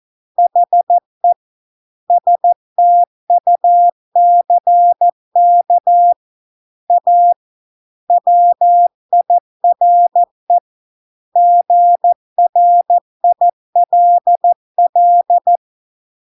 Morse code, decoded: HE STUCK A WIRE GRILL

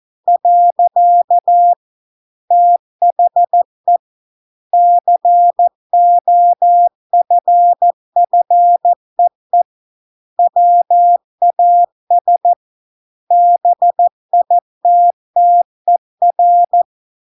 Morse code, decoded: . THE COFFEE WAS BITTER

